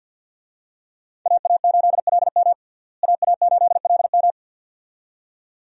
Morse code, heard as a keyed tone, about 52 wpm, 700 Hz. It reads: AA8LM AA8LM
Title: USA Call Signs - 5 Characters - Repeated 2x - 50wpm